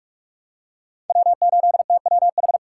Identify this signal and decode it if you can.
Morse code, heard as a keyed tone, about 41 wpm, 700 Hz. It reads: W8TWH
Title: USA Call Signs - 5 Characters - 2x Character Spacing - 45wpm